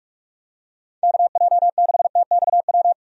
Morse code, heard as a keyed tone, about 45 wpm, 700 Hz. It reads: KJ6TXW
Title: USA Call Signs - 6 Characters - 45wpm